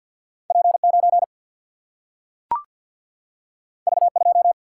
Morse code, heard as a keyed tone, about 52 wpm, 700 Hz. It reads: P9 V2
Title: Single-Letter Single-Number 50wpm